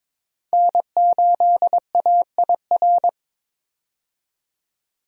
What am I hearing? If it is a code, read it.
Morse code, heard as a keyed tone, about 22 wpm, 700 Hz. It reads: N8AIR
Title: USA Call Signs - 5 Characters - 22wpm